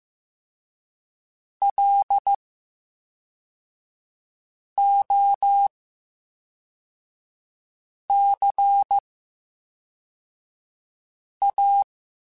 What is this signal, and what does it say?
Morse code, heard as a keyed tone, about 15 wpm, 800 Hz. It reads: L O C A
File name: Special INGs / Special INGs-mp3s / VerySlowToFaster Ing @10